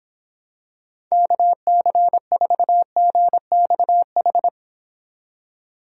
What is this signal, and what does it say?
Morse code, heard as a keyed tone, about 26 wpm, 700 Hz. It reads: KC4GXH